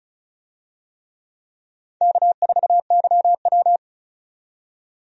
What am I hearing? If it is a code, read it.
Morse code, heard as a keyed tone, about 36 wpm, 700 Hz. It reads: K4YW
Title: USA Call Signs - 4 Characters - 35wpm